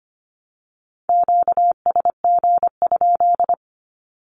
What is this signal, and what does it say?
Morse code, heard as a keyed tone, about 25 wpm, 700 Hz. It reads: QSG?
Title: Q-Codes 25wpm